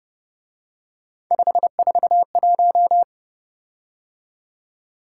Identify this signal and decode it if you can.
Morse code, heard as a keyed tone, about 29 wpm, 700 Hz. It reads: 541